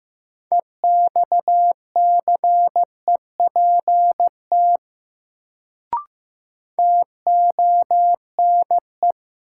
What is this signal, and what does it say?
Morse code, heard as a keyed tone, about 15 wpm, 700 Hz. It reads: EXCEPT TONE
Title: Sentences from Top 1600 Words - New Word Drill - 15wpm